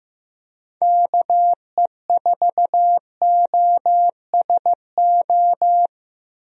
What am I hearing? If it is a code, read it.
Morse code, heard as a keyed tone, about 15 wpm, 700 Hz. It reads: KE4OSO
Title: USA Call Signs - 6 Characters - Repeated 3x - 15wpm